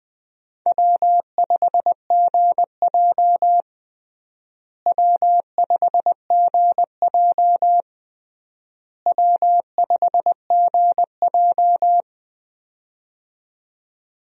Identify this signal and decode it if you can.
Morse code, heard as a keyed tone, about 20 wpm, 700 Hz. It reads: W5GJ W5GJ W5GJ
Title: USA Call Signs - 4 Characters - Repeated 3x - 20wpm